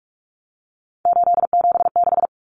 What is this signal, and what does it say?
Morse code, heard as a keyed tone, about 44 wpm, 700 Hz. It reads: 876
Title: Common Cut Numbers - 3-Digit 45wpm